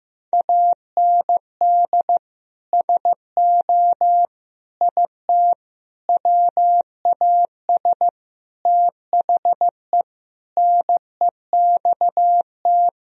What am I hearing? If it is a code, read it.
Morse code, heard as a keyed tone, about 15 wpm, 700 Hz. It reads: AND SO IT WAS THE NEXT
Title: Sentences from Top 1000 Words - Review Part 1 - 15wpm